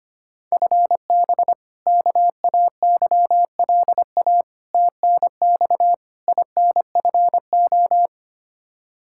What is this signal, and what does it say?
Morse code, heard as a keyed tone, about 25 wpm, 700 Hz. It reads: FB KAYLA TNX INFO
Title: K1USN SST QSO Practice 25wpm